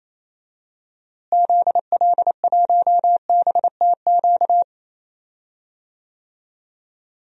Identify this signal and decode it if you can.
Morse code, heard as a keyed tone, about 28 wpm, 700 Hz. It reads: ZL1BTQ
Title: DXCC-100 Call Signs - 6 Characters - Repeated 2x - 28wpm